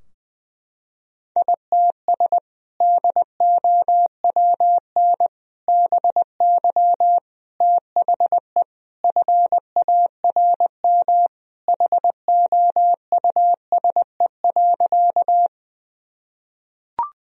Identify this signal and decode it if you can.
Morse code, heard as a keyed tone, about 21 wpm, 700 Hz. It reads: ITS DOWN BY THE FARM HOUSE.
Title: Sentences from Top 1000 Words - Review Part 2 - 20wpm